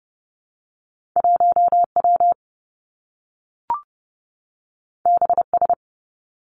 Morse code, heard as a keyed tone, about 30 wpm, 700 Hz. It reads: 1 W B S
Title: Two Challenging Letter-Number Combinations 30wpm